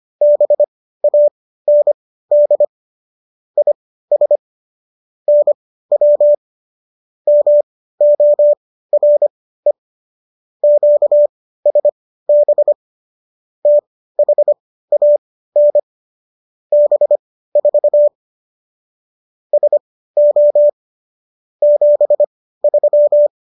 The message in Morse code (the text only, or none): BAND IS NW MORE QSB THAN B4 SO 73